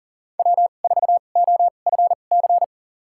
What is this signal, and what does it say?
Morse code, heard as a keyed tone, about 40 wpm, spaced as for 29 wpm, 700 Hz. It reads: W4OFC
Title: USA Call Signs - 5 Characters - 2x Character Spacing - 40wpm